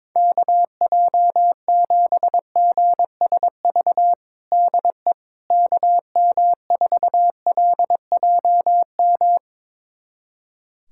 Morse code, heard as a keyed tone, about 22 wpm, 700 Hz. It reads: KJ7GSV DE KM4LJM
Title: USA Callsign DE Callsign - 6 Characters - 22wpm